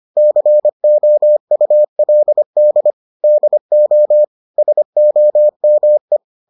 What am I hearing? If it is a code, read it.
Morse code, heard as a keyed tone, about 25 wpm, 600 Hz. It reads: COULD DO SOME